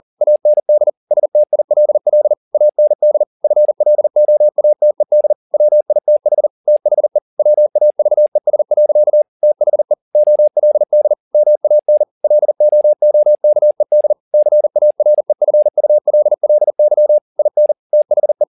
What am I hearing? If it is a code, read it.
Morse code, heard as a keyed tone, about 40 wpm, 600 Hz. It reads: AND STILL AND FLOATED WITH THE WAVES. THE OLD MAN LOOKED CAREFULLY IN THE
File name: the old man and the sea 2 40wpm